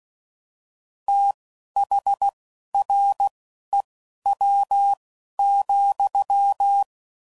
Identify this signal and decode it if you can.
Morse code, heard as a keyed tone, about 16 wpm, spaced as for 11 wpm, 800 Hz. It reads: THREW,
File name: Prefixes / FLs & THs / Fast TH Prefixes